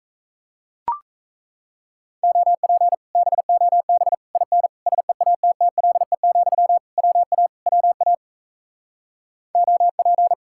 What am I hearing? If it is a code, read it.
Morse code, heard as a keyed tone, about 37 wpm, 700 Hz. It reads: OP BOB IN SEATTLE, WA WA OP